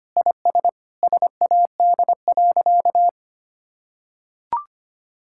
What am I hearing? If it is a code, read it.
Morse code, heard as a keyed tone, about 25 wpm, 700 Hz. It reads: IS SAD.
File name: practice / 25wpm / Sentences with 2-Word Phrases - 25wpm